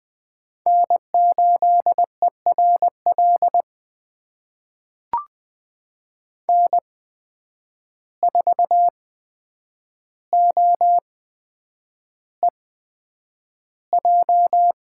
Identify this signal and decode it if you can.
Morse code, heard as a keyed tone, about 20 wpm, spaced as for 9 wpm, 700 Hz. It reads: N8ERL N4OEJ
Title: USA Call Signs - 5 Characters - 8x Character Spacing - 20wpm